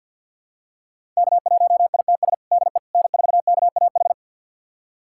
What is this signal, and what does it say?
Morse code, heard as a keyed tone, about 52 wpm, 700 Hz. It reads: K1ITS DE N4KAH
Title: USA Callsign DE Callsign - 5 Characters - 50wpm